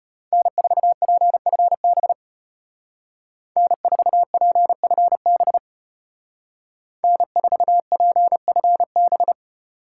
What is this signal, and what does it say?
Morse code, heard as a keyed tone, about 33 wpm, 700 Hz. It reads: N4PFB N4PFB N4PFB